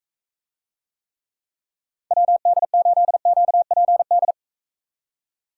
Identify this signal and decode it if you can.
Morse code, heard as a keyed tone, about 40 wpm, 700 Hz. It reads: WD8QPD